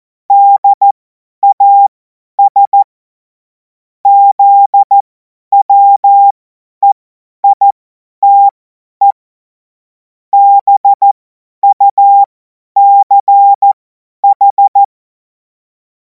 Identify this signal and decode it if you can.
Morse code, heard as a keyed tone, about 14 wpm, spaced as for 10 wpm, 800 Hz. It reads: DAS ZWEITE BUCH
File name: Morsekurs / Witze / Witz 05. Mai 2026 50 F70